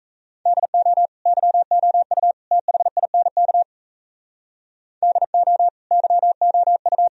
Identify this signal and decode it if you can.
Morse code, heard as a keyed tone, about 34 wpm, 700 Hz. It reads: DO YOU THINK DO YOU
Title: Speed-Racing - Sets of 3 Words 28wpm